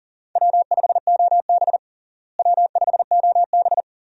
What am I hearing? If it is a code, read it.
Morse code, heard as a keyed tone, about 40 wpm, 700 Hz. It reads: W5OB W5OB
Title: USA Call Signs - 4 Characters - Repeated 2x - 40wpm